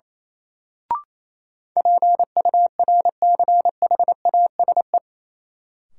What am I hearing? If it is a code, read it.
Morse code, heard as a keyed tone, about 28 wpm, 700 Hz. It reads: PURCHASE
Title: Sentences from Top 800 Words - New Word Drill - 28wpm